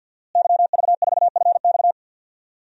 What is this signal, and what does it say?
Morse code, heard as a keyed tone, about 52 wpm, 700 Hz. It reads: YV4FX
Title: DXCC-100 Call Signs - 5 Characters - Repeated 3x - 50wpm